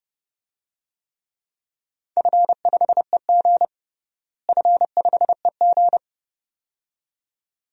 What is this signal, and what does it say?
Morse code, heard as a keyed tone, about 30 wpm, 700 Hz. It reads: F5EG F5EG
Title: DXCC-100 Call Signs - 4 Characters - Repeated 2x - 30wpm